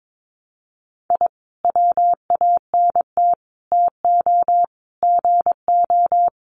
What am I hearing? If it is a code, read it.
Morse code, heard as a keyed tone, about 22 wpm, 700 Hz. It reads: I WANT TO GO
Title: Sets of 5 Words Encore 22wpm